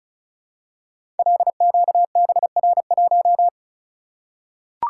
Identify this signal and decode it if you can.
Morse code, heard as a keyed tone, about 36 wpm, 700 Hz. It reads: LQBR1
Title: Code Groups - 5 Letter:Numbers - 35wpm